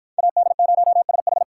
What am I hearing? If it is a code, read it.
Morse code, heard as a keyed tone, about 49 wpm, 700 Hz. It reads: AD0SH